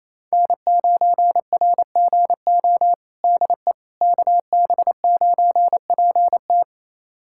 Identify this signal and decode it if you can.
Morse code, heard as a keyed tone, about 28 wpm, 700 Hz. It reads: N9RGO DE KB9PT
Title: USA Callsign DE Callsign - 5 Characters - 28wpm